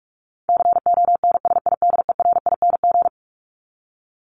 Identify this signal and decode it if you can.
Morse code, heard as a keyed tone, about 44 wpm, 700 Hz. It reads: CONSIDERING
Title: ING Suffix - Top 100 Words with Base Words 45wpm